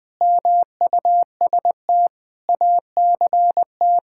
Morse code, heard as a keyed tone, about 20 wpm, 700 Hz. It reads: MUST ACT
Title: Sentences with 9-Word Phrases - 20wpm